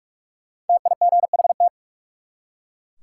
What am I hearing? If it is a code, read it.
Morse code, heard as a keyed tone, about 44 wpm, 700 Hz. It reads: TIGHT
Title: Five Letter Words - 45wpm